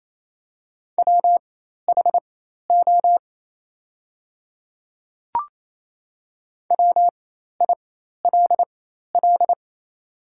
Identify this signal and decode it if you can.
Morse code, heard as a keyed tone, about 27 wpm, spaced as for 13 wpm, 700 Hz. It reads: WHO WILL
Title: 4x Character Spacing - Top 100 Words - 28wpm